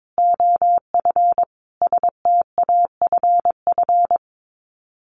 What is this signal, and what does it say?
Morse code, heard as a keyed tone, about 22 wpm, 700 Hz. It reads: OF STAFF
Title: Sets of 5 Words Encore2 22wpm